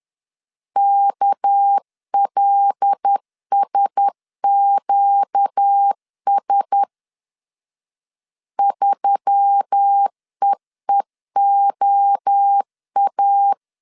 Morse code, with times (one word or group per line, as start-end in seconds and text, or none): 0.8-6.8 KLSQS
8.6-13.5 3EEOA